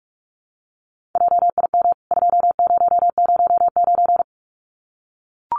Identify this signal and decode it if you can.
Morse code, heard as a keyed tone, about 44 wpm, 700 Hz. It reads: JIM 2009
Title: CWT Exchange 45wpm